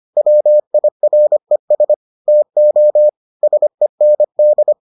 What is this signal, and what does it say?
Morse code, heard as a keyed tone, about 25 wpm, 600 Hz. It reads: WIRES TO SEND